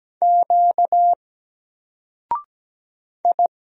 Morse code, heard as a keyed tone, about 17 wpm, 700 Hz. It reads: Q I
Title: Single Letter-Number 17wpm